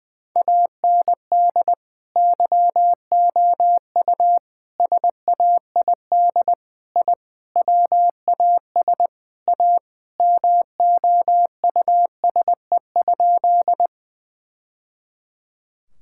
Morse code, heard as a keyed tone, about 20 wpm, 700 Hz. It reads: AND YOU SAID I WAS A MOUSE?
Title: Sentences from Top 1000 Words - Review Part 5 - 20wpm